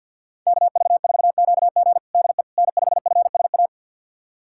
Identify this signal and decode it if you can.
Morse code, heard as a keyed tone, about 51 wpm, 700 Hz. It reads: KV4QG DE N5FSA
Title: USA Callsign DE Callsign - 5 Characters - 50wpm